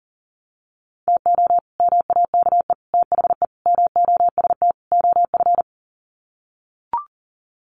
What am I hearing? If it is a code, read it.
Morse code, heard as a keyed tone, about 37 wpm, 700 Hz. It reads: TO MAKE THE MOST OF
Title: Sets of 5 Words Encore2 40wpm